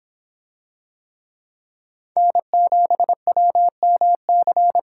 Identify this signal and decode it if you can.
Morse code, heard as a keyed tone, about 24 wpm, 700 Hz. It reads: N7WMC